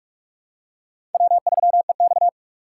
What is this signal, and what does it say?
Morse code, heard as a keyed tone, about 45 wpm, 700 Hz. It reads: W3EX
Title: USA Call Signs - 3x Character Spacing - 45wpm